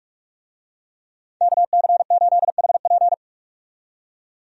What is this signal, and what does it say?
Morse code, heard as a keyed tone, about 43 wpm, 700 Hz. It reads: KC8HP
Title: USA Call Signs - 5 Characters - 45wpm